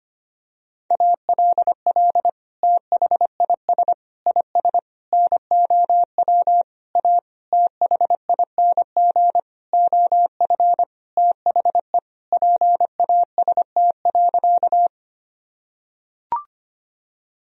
Morse code, heard as a keyed tone, about 25 wpm, 700 Hz. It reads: ALL THIS IS NOW A THING OF THE PAST.
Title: Sentences from Top 1000 Words - Review Part 3 - 25wpm